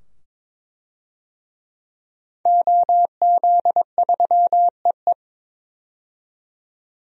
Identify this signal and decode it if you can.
Morse code, heard as a keyed tone, about 22 wpm, 700 Hz. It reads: OZ3EE
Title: DXCC-100 Call Signs - 5 Characters - 22wpm